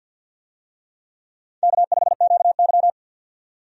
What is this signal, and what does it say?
Morse code, heard as a keyed tone, about 51 wpm, 700 Hz. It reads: K5QY
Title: USA Call Signs - 4 Characters - 50wpm